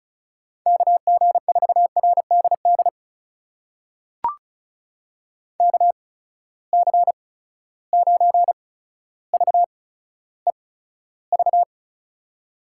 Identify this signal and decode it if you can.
Morse code, heard as a keyed tone, about 36 wpm, spaced as for 15 wpm, 700 Hz. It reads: KG4RDD KC9VEV
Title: USA Call Signs - 6 Characters - 8x Character Spacing - 35wpm